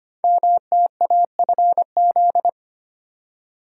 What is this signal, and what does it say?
Morse code, heard as a keyed tone, about 25 wpm, 700 Hz. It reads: MTAFZ
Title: Code Groups - 5 Letters - 25wpm